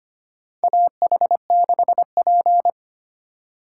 Morse code, heard as a keyed tone, about 25 wpm, 700 Hz. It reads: AH6P
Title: USA Call Signs - 4 Characters - 6x Character Spacing - 25wpm